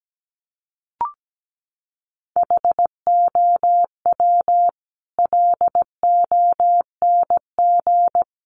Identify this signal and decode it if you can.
Morse code, heard as a keyed tone, about 17 wpm, 700 Hz. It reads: HOW LONG
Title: Sets of 5 Words Encore 17wpm